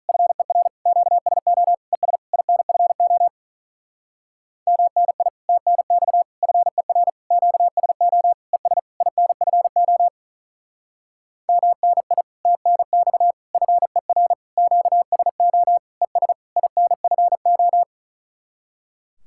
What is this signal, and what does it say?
Morse code, heard as a keyed tone, about 42 wpm, 700 Hz. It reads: FER QSO ES INFO MNI TNX FER QSO ES INFO MNI TNX FER QSO ES INFO
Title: Speed-Racing Common QSO Phrases Encore - 35wpm